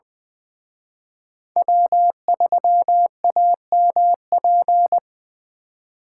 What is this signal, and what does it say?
Morse code, heard as a keyed tone, about 24 wpm, 700 Hz. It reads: W3AMP
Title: Speed Racing - USA 1x3 Call Signs 20wpm